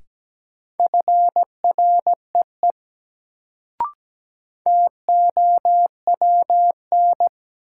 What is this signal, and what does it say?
Morse code, heard as a keyed tone, about 17 wpm, 700 Hz. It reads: FREE TOWN
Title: Sentences from Top 500 Words - New Word Drill - 17wpm